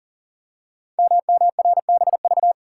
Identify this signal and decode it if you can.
Morse code, heard as a keyed tone, about 43 wpm, 700 Hz. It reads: MMRBV
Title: Code Groups - 5 Letters - 40wpm